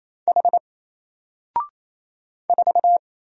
Morse code, heard as a keyed tone, about 27 wpm, 700 Hz. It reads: H 4
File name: Single-Letter-Number-28wpm